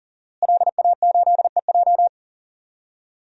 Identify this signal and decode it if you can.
Morse code, heard as a keyed tone, about 40 wpm, 700 Hz. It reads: LA8EJ
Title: DXCC-100 Call Signs - 5 Characters - Repeated 2x - 40wpm